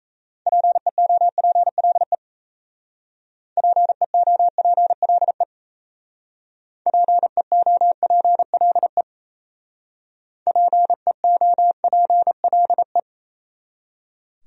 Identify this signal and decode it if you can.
Morse code, heard as a keyed tone, about 36 wpm, 700 Hz. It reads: PEOPLE PEOPLE PEOPLE PEOPLE